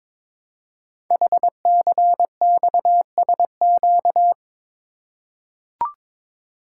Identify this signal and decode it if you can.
Morse code, heard as a keyed tone, about 22 wpm, 700 Hz. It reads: HCXSQ
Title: Code Groups - 5 Letters - 22wpm